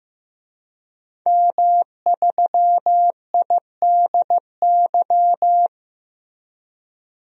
Morse code, heard as a keyed tone, about 15 wpm, 700 Hz. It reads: M3IDY
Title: DXCC-100 Call Signs - 5 Characters - 15wpm